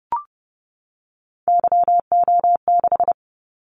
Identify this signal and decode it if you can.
Morse code, heard as a keyed tone, about 30 wpm, 700 Hz. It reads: YO6
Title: Callsign Prefixes for Top 25 DXCC - 30wpm